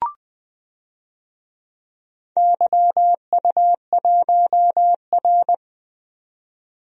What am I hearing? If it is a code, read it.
Morse code, heard as a keyed tone, about 20 wpm, 700 Hz. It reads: YU1R
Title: DXCC-100 Call Signs - 4 Characters - 20wpm